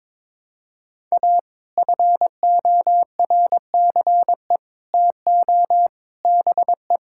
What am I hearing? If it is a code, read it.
Morse code, heard as a keyed tone, about 21 wpm, 700 Hz. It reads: A FORCE TO BE
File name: practice / 22wpm / Sets of 8 Words - 22wpm